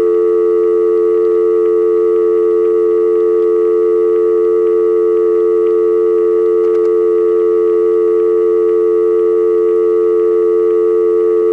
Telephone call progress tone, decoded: dial tone